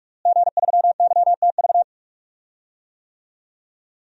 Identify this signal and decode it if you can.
Morse code, heard as a keyed tone, about 41 wpm, 700 Hz. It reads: G3YTV
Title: DXCC-100 Call Signs - 5 Characters - Repeated 3x - 45wpm